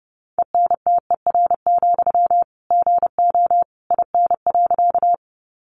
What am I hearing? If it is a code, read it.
Morse code, heard as a keyed tone, about 30 wpm, 700 Hz. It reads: ENTER, GO IN.
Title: Sentences from Top 1300 Words 30wpm